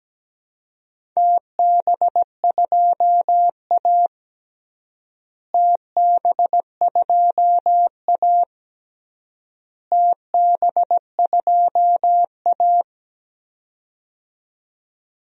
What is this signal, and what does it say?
Morse code, heard as a keyed tone, about 17 wpm, 700 Hz. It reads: TB2A TB2A TB2A
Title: DXCC-100 Call Signs - 4 Characters - Repeated 3x - 17wpm